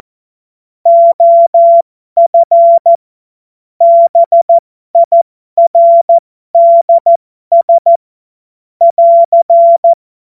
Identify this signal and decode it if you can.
Morse code, heard as a keyed tone, about 14 wpm, spaced as for 12 wpm, 680 Hz. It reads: OF BIRDS <AR>
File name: Morsekurs / Witze